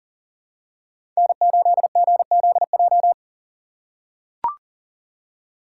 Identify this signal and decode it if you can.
Morse code, heard as a keyed tone, about 38 wpm, 700 Hz. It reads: N8GZJ
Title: USA Call Signs - 5 Characters - 6x Character Spacing - 40wpm